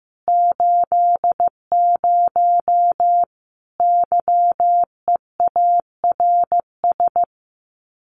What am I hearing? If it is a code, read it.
Morse code, heard as a keyed tone, about 15 wpm, 700 Hz. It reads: 80 YEARS
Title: Common QSO Phrases 15wpm